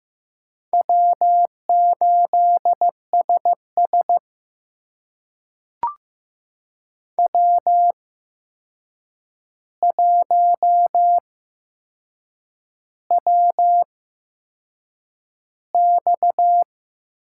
Morse code, heard as a keyed tone, about 15 wpm, spaced as for 6 wpm, 700 Hz. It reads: W8SS W1WX
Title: USA Call Signs - 4 Characters - 8x Character Spacing - 15wpm